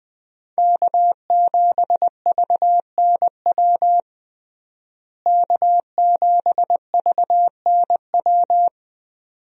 Morse code, heard as a keyed tone, about 21 wpm, 700 Hz. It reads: K7VNW K7VNW
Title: USA Call Signs - 5 Characters - Repeated 2x - 20wpm